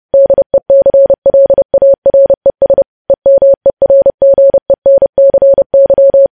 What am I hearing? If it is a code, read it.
Morse code, heard as a keyed tone, about 30 wpm, 550 Hz. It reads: DECLARES EMERGENCY